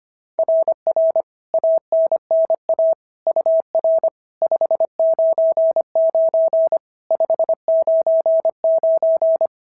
Morse code, heard as a keyed tone, about 22 wpm, 650 Hz. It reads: RR ANNA UR 599 599
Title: SOTA QSO Practice 25wpm